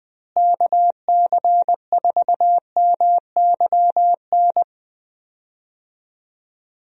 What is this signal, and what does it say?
Morse code, heard as a keyed tone, about 20 wpm, 700 Hz. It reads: KC4MYN
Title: USA Call Signs - 6 Characters - Repeated 2x - 20wpm